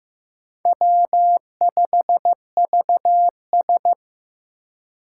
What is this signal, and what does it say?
Morse code, heard as a keyed tone, about 15 wpm, 700 Hz. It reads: W5VS
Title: USA Call Signs - 4 Characters - 3x Character Spacing - 15wpm